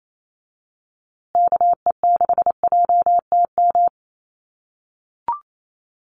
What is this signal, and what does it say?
Morse code, heard as a keyed tone, about 28 wpm, 700 Hz. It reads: KE6JTM
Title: USA Call Signs Repeated 3x 28wpm